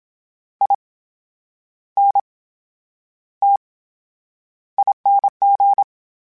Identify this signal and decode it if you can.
Morse code, heard as a keyed tone, about 27 wpm, spaced as for 9 wpm, 800 Hz. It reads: INTING
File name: Special INGs / Special INGs-mp3s / VerySlowToFaster Ing @25